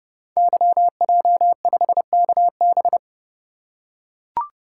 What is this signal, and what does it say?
Morse code, heard as a keyed tone, about 30 wpm, 700 Hz. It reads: YJ5KB
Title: Code Groups - 5 Letter:Numbers - 30wpm